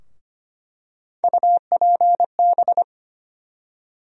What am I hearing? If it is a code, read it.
Morse code, heard as a keyed tone, about 25 wpm, 700 Hz. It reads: UPB
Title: USA Call Sign Postfixes - 3 Letters 25wpm